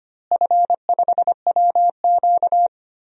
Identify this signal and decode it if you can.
Morse code, heard as a keyed tone, about 25 wpm, 700 Hz. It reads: F5WQ
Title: DXCC-100 Call Signs - 4 Characters - Repeated 3x - 25wpm